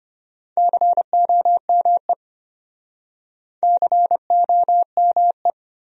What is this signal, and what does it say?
Morse code, heard as a keyed tone, about 36 wpm, 700 Hz. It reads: COME COME